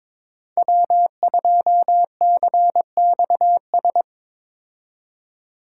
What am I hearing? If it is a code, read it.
Morse code, heard as a keyed tone, about 22 wpm, 700 Hz. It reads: W2CXS
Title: Code Groups - 5 Letter:Numbers - 22wpm